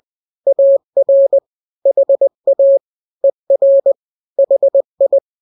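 Morse code, heard as a keyed tone, about 20 wpm, 550 Hz. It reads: AR HA ER HI
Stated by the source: Morse drill of two-letter n-grams at 20/18 WPM (Farnsworth), from KK7RR.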